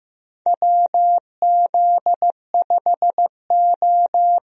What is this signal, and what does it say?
Morse code, heard as a keyed tone, about 15 wpm, 700 Hz. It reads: WZ5O
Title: USA Call Signs - 4 Characters - Repeated 3x - 15wpm